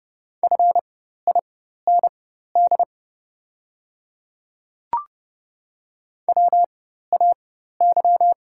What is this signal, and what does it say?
Morse code, heard as a keyed tone, about 30 wpm, spaced as for 14 wpm, 700 Hz. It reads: FIND WAY
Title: 4x Character Spacing - Top 100 Words - 30wpm